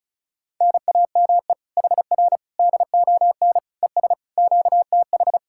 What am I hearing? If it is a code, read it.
Morse code, heard as a keyed tone, about 43 wpm, 700 Hz. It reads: NAME HR DON ES QTH
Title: Speed-Racing Common QSO Phrases - 35wpm